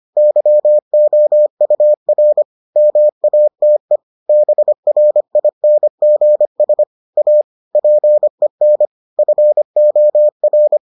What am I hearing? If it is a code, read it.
Morse code, heard as a keyed tone, about 25 wpm, 600 Hz. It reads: YOUR MATE BRINGS A PEN FOR